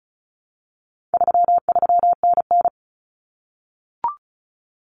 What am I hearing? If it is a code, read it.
Morse code, heard as a keyed tone, about 34 wpm, 700 Hz. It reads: 33NN
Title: Common Cut Numbers - 4-Digit 35wpm